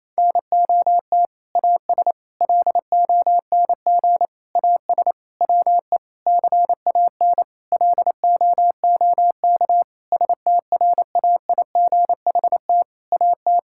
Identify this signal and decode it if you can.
Morse code, heard as a keyed tone, about 28 wpm, 700 Hz. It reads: NOT AS LONG AS WE CAN LOOK STRAIGHT AT
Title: Sentences from Top 1000 Words - Review Part 6 - 28wpm